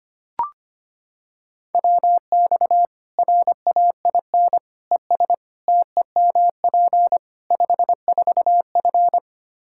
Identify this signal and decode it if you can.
Morse code, heard as a keyed tone, about 25 wpm, 700 Hz. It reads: WX RAIN ES TEMP 54F
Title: Practice QSO - Exchange 2 - Info - 25wpm